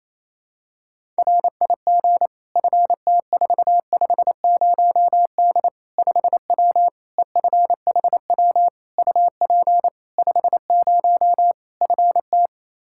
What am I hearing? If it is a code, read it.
Morse code, heard as a keyed tone, about 28 wpm, 700 Hz. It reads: RIG FT450D 5W EFHW UP 50 FT
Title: Practice QSO - Exchange 2 - Info - 28wpm